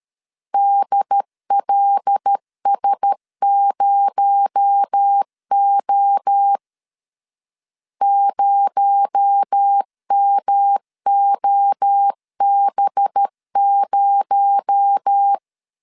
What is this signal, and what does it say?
Morse code, heard as a keyed tone, about 13 wpm, 800 Hz. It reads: DLS0O 0MOB0
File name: LEZIONI / 6 PKX0 / 6 PKX0 12wpm 2231